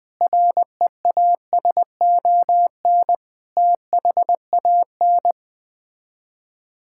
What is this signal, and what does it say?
Morse code, heard as a keyed tone, about 21 wpm, 700 Hz. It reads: REASON THAN
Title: Sets of 6 Words - 20wpm